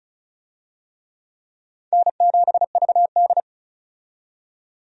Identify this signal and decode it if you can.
Morse code, heard as a keyed tone, about 36 wpm, 700 Hz. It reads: N7VD
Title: USA Call Signs - 4 Characters - 35wpm